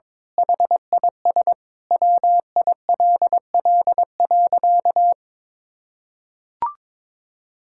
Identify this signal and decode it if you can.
Morse code, heard as a keyed tone, about 22 wpm, 700 Hz. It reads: HIS WILL.
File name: Sentences with 9-Word Phrases - 22wpm